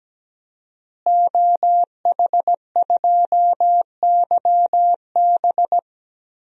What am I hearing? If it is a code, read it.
Morse code, heard as a keyed tone, about 17 wpm, 700 Hz. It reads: OH2YB
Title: DXCC-100 Call Signs - 5 Characters - Repeated 2x - 17wpm